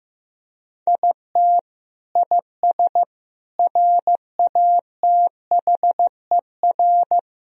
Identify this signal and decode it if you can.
Morse code, heard as a keyed tone, about 15 wpm, 700 Hz. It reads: IT IS RATHER
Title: Sentences from Top 1000 Words - Review Part 4 - 15wpm